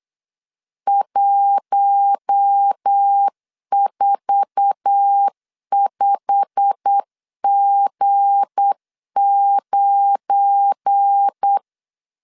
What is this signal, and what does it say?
Morse code, heard as a keyed tone, about 8 wpm, 800 Hz. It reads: 145G9
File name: CW / DET / 1 9 1500C S C 8wpm 5653